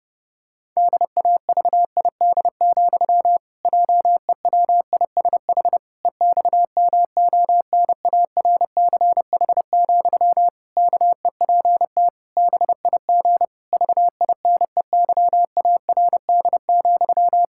Morse code, heard as a keyed tone, about 30 wpm, 700 Hz. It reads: DAVID, JEWISH EXMONARCH, KEPT BIG VINEYARD,
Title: Pangram Challenge v2 - 30wpm